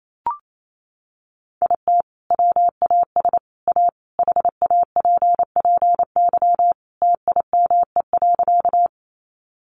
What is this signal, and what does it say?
Morse code, heard as a keyed tone, about 28 wpm, 700 Hz. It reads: IT WAS A HAPPY TIME.
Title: Sentences from Top 1000 Words 28wpm